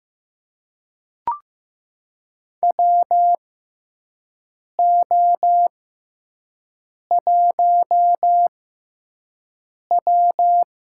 Morse code, heard as a keyed tone, about 15 wpm, spaced as for 5 wpm, 700 Hz. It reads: WO1W
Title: USA Call Signs - 4 Characters - 6x Character Spacing - 15wpm